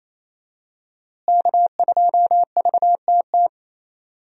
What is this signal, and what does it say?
Morse code, heard as a keyed tone, about 28 wpm, 700 Hz. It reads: K2VTT
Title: USA Call Signs - 5 Characters - Repeated 2x - 28wpm